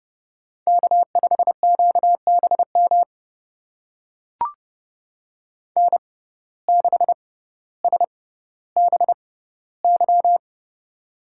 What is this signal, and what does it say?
Morse code, heard as a keyed tone, about 28 wpm, spaced as for 15 wpm, 700 Hz. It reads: K5QBM N6SBY
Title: USA Call Signs - 5 Characters - 6x Character Spacing - 30wpm